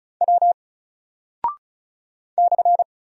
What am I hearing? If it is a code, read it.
Morse code, heard as a keyed tone, about 36 wpm, 700 Hz. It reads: W /
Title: Single Character (Letter:Number:Punctuation:Prosign) Alt 35wpm